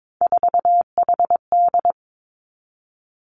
Morse code, heard as a keyed tone, about 22 wpm, 700 Hz. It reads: 4HD
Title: Three Challenging Letter-Number Combinations 22wpm